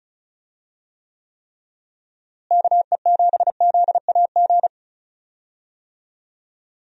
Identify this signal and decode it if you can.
Morse code, heard as a keyed tone, about 36 wpm, 700 Hz. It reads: KE7ZAG
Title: USA Call Signs - 6 Characters - Repeated 3x - 35wpm